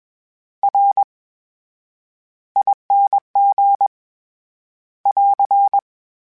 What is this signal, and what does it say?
Morse code, heard as a keyed tone, about 21 wpm, spaced as for 10 wpm, 800 Hz. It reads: RING <AR>
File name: Special INGs-mp3s / Special INGs-mp3s / VerySlowToFaster Ing @20